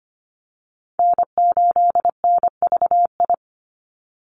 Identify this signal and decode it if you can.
Morse code, heard as a keyed tone, about 25 wpm, 700 Hz. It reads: N8NVI
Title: USA Call Signs 25wpm